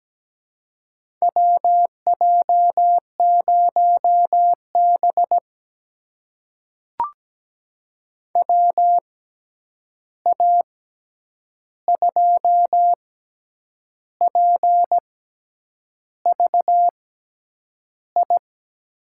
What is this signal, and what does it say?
Morse code, heard as a keyed tone, about 17 wpm, spaced as for 8 wpm, 700 Hz. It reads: WJ0B WA2PVI
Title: USA Call Signs - 6x Character Spacing - 17wpm